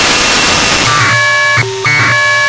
Packet radio (AX.25, as in APRS).
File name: Sohla-1 23h48 utc 1200bd packet NormalModeby ja0caw